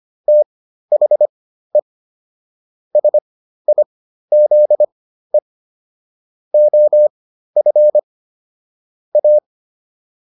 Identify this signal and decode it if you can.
Morse code, heard as a keyed tone, about 25 wpm, spaced as for 13 wpm, 600 Hz. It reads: THE SIZE OF A
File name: WD302 13